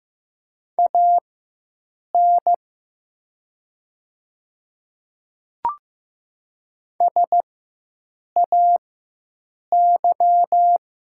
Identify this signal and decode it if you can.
Morse code, heard as a keyed tone, about 15 wpm, spaced as for 7 wpm, 700 Hz. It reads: AN SAY
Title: 4x Character Spacing - Top 200 Words - 15wpm